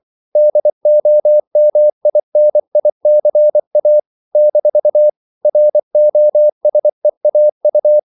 Morse code, heard as a keyed tone, about 24 wpm, 600 Hz. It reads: DOMINICA - ROSEAU
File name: CW / Practice / NationalCapitals24